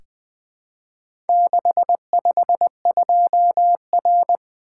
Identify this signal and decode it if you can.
Morse code, heard as a keyed tone, about 20 wpm, 700 Hz. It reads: 652R
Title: Four Letter-Number Combinations - 20wpm